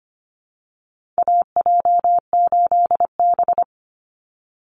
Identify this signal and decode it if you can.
Morse code, heard as a keyed tone, about 25 wpm, 700 Hz. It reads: AJ8B
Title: Call Signs - 25wpm